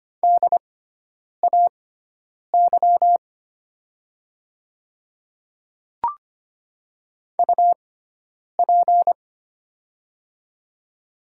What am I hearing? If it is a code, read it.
Morse code, heard as a keyed tone, about 25 wpm, spaced as for 9 wpm, 700 Hz. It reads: DAY UP
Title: 6x Character Spacing - Top 100 Words - 25wpm